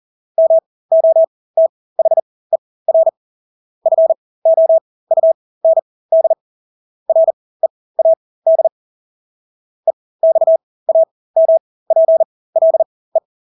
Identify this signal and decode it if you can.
Morse code, heard as a keyed tone, about 40 wpm, spaced as for 20 wpm, 650 Hz. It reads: MOTHER FOUND READ EXAMPLE
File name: Engelske ord 40-20 wpm